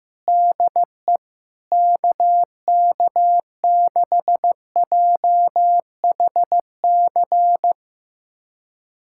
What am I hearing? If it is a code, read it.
Morse code, heard as a keyed tone, about 15 wpm, 700 Hz. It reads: DE KK6JHC
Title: USA Callsign DE Callsign 15wpm